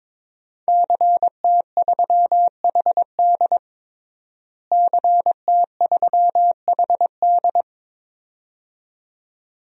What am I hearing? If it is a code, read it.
Morse code, heard as a keyed tone, about 22 wpm, 700 Hz. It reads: CT3HD CT3HD
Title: DXCC-100 Call Signs - 5 Characters - Repeated 2x - 22wpm